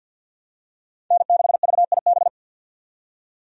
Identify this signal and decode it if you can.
Morse code, heard as a keyed tone, about 53 wpm, 700 Hz. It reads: N6VIB
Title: USA Call Signs - 5 Characters - 50wpm